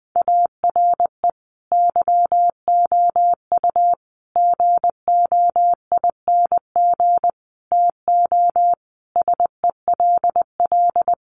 Morse code, heard as a keyed tone, about 20 wpm, 700 Hz. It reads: ARE YOU GOING TO SELL
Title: Sentences from Top 1200 Words 20wpm